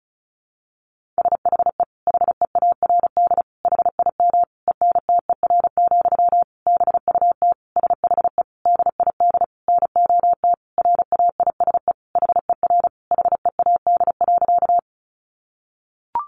Morse code, heard as a keyed tone, about 34 wpm, 700 Hz. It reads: SHE HEARD HIM ENTER, BUT SHE DID NOT RAISE HER HEAD.
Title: Sentences from Top 1400 Words 35wpm